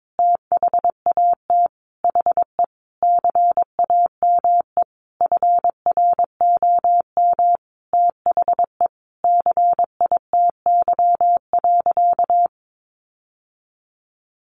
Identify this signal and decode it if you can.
Morse code, heard as a keyed tone, about 22 wpm, 700 Hz. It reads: THAT HE CAME FROM THE CITY.
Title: Sentences from Top 1500 Words 22wpm